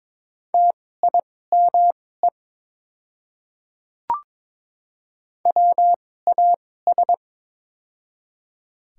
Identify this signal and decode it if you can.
Morse code, heard as a keyed tone, about 22 wpm, spaced as for 16 wpm, 700 Hz. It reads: TIME WAS
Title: 2x Character Spacing - Top 200 Words - 22wpm